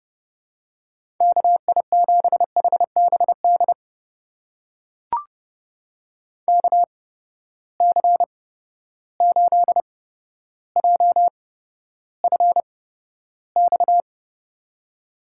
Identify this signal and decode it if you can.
Morse code, heard as a keyed tone, about 30 wpm, spaced as for 13 wpm, 700 Hz. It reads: KI7HBD KC8JFX
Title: USA Call Signs - 6 Characters - 8x Character Spacing - 30wpm